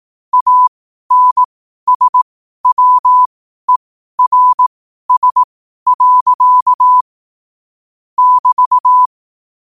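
Morse code, heard as a keyed tone, about 18 wpm, spaced as for 12 wpm, 1000 Hz. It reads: ANSWERS. <BT>
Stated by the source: Morse code sent at 12 words per minute.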